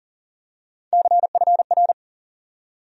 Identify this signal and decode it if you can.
Morse code, heard as a keyed tone, about 40 wpm, 700 Hz. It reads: CFR
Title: USA Call Sign Postfixes - 3 Letters 40wpm